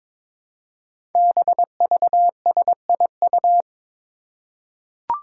Morse code, heard as a keyed tone, about 22 wpm, 700 Hz. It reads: BVSIU
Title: Five Challenging Letter-Number Combinations 22wpm